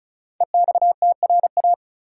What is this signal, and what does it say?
Morse code, heard as a keyed tone, about 36 wpm, 700 Hz. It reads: EXTRA
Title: Sentences from Top 1000 Words - New Word Drill - 35wpm